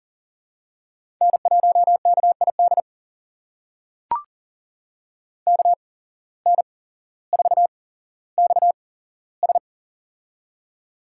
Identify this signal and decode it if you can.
Morse code, heard as a keyed tone, about 39 wpm, spaced as for 17 wpm, 700 Hz. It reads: N1KID KN4XS
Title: USA Call Signs - 5 Characters - 8x Character Spacing - 40wpm